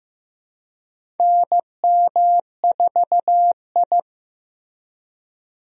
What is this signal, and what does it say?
Morse code, heard as a keyed tone, about 15 wpm, 700 Hz. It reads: NM4I
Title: USA Call Signs - 4 Characters - Repeated 3x - 15wpm